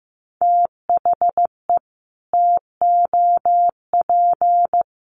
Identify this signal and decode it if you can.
Morse code, heard as a keyed tone, about 14 wpm, 700 Hz. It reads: THE TOP
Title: Sets of 5 Words Encore2 15wpm